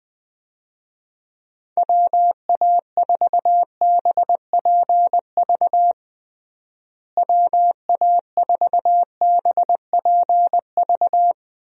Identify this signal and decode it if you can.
Morse code, heard as a keyed tone, about 21 wpm, 700 Hz. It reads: WA4BPV WA4BPV
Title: USA Call Signs - 6 Characters - Repeated 3x - 20wpm